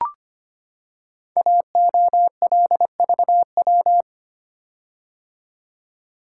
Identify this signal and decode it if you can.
Morse code, heard as a keyed tone, about 25 wpm, 690 Hz. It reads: AOLVW
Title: Code Groups - 5 Letters - 25wpm